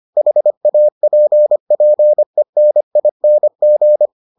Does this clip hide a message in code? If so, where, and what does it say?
Yes: Morse code from 0.2 to 4.1 s, heard as a keyed tone, about 25 wpm, 600 Hz. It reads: HAPPENING